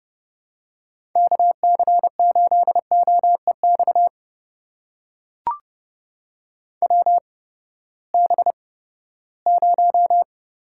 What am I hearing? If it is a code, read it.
Morse code, heard as a keyed tone, about 30 wpm, spaced as for 17 wpm, 700 Hz. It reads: KC8OEX WB0
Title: USA Call Signs - 6 Characters - 8x Character Spacing - 30wpm